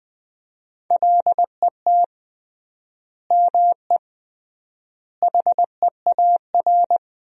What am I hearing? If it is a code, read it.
Morse code, heard as a keyed tone, about 21 wpm, 700 Hz. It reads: LET ME HEAR
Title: 3x Word-Spacing - Sentences from Top 300 Words - 20wpm